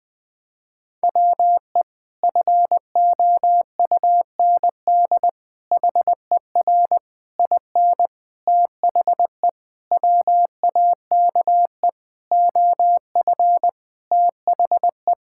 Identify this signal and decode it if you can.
Morse code, heard as a keyed tone, about 20 wpm, 700 Hz. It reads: WE FOUND HER IN THE WAKE OF THE
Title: Sentences with 4-Word Phrases - Spoken Beginning - 20wpm